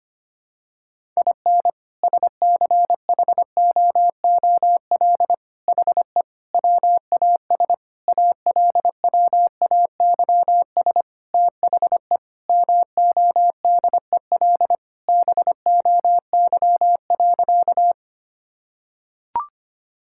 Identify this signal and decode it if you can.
Morse code, heard as a keyed tone, about 25 wpm, 700 Hz. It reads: IN SCHOOL HE WAS ALWAYS THE MODEL BOY.
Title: Sentences from Top 1000 Words - Review Part 5 - 25wpm